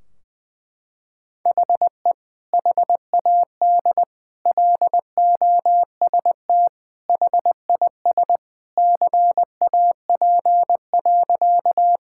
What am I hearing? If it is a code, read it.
Morse code, heard as a keyed tone, about 20 wpm, 700 Hz. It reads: HE HAD LOST HIS CAP.
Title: Sentences from Top 1600 Words 20wpm